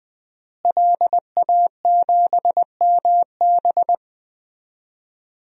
Morse code, heard as a keyed tone, about 20 wpm, 700 Hz. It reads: LA7MB
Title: DXCC-100 Call Signs - 5 Characters - Repeated 2x - 20wpm